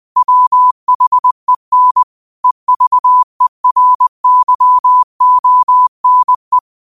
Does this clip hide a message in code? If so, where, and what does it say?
Yes: Morse code from 0.2 to 6.6 s, heard as a keyed tone, about 20 wpm, 1000 Hz. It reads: WHEN EVERYONE